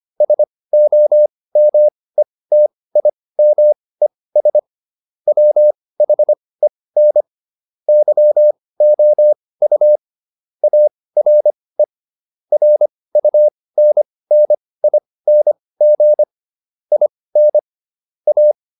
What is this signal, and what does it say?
Morse code, heard as a keyed tone, about 25 wpm, spaced as for 18 wpm, 600 Hz. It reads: SOMETIMES WHEN YOU ARE RUNNING IN A